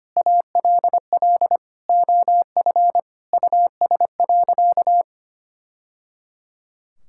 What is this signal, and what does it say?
Morse code, heard as a keyed tone, about 25 wpm, 700 Hz. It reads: ALL OF US.
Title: Sentences from Top 1000 Words - Review Part 5 - 25wpm